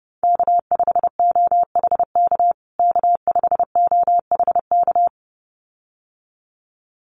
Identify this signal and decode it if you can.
Morse code, heard as a keyed tone, about 30 wpm, 700 Hz. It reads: K5OHK K5OHK
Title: USA Call Signs Repeated 2x 30wpm